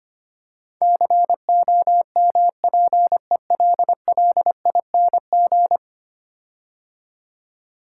Morse code, heard as a keyed tone, about 25 wpm, 700 Hz. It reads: COMPELLING
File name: 10 Letter Words - First 3 Letters Given - 25wpm